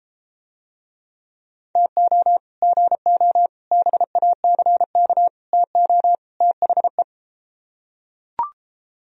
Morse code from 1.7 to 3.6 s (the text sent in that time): TO GO